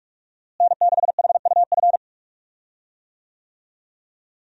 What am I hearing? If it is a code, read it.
Morse code, heard as a keyed tone, about 46 wpm, 700 Hz. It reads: N6HUF
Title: USA Call Signs - 5 Characters - Repeated 2x - 45wpm